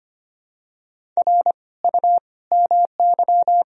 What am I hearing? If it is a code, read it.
Morse code, heard as a keyed tone, about 25 wpm, 700 Hz. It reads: R U MY
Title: Sentences from Top 300 Words and Abbreviations 25wpm